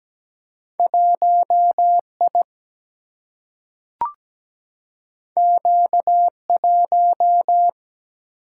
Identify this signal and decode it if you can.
Morse code, heard as a keyed tone, about 17 wpm, 700 Hz. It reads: 1 I Q 1
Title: Single-Letter Single-Number 17wpm